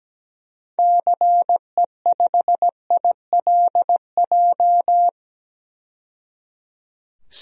Morse code, heard as a keyed tone, about 17 wpm, 700 Hz. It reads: CE5ILJ
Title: DXCC-100 Call Signs - 6 Characters - 17wpm